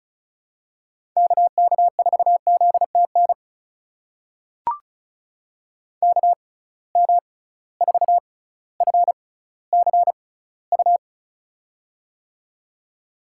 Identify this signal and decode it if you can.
Morse code, heard as a keyed tone, about 34 wpm, spaced as for 18 wpm, 700 Hz. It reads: KK4ZTN KM4FCU
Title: USA Call Signs - 6 Characters - 6x Character Spacing - 35wpm